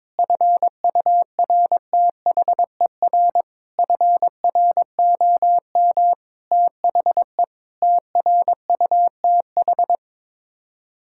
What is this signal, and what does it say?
Morse code, heard as a keyed tone, about 22 wpm, 700 Hz. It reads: FURTHER FROM THE TRUTH
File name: practice / 22wpm / Sets of 8 Words - 22wpm